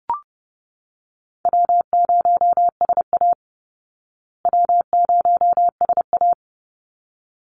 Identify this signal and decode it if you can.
Morse code, heard as a keyed tone, about 30 wpm, 700 Hz. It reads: W0SA W0SA
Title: Easy Call Signs Repeated 2x - 30wpm